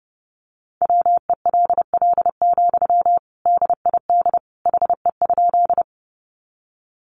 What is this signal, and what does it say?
Morse code, heard as a keyed tone, about 30 wpm, 700 Hz. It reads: WELL, DID HE?
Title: Sentences from Top 200 Words 30wpm